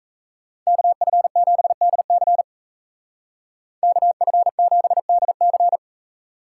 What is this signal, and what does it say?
Morse code, heard as a keyed tone, about 35 wpm, 700 Hz. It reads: KF7DC KF7DC
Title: Speed Racing - USA 2x2 Call Signs 28wpm